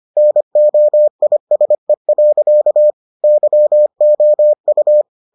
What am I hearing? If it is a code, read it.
Morse code, heard as a keyed tone, about 25 wpm, 600 Hz. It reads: NOISE. YOU